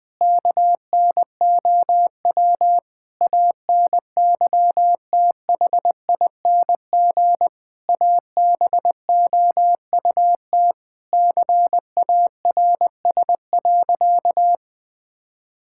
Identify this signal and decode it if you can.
Morse code, heard as a keyed tone, about 20 wpm, 700 Hz. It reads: KNOW ANYTHING ABOUT CARS.
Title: Sentences with 5-Word Phrases - 20wpm